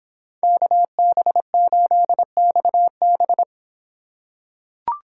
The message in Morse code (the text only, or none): KB8XB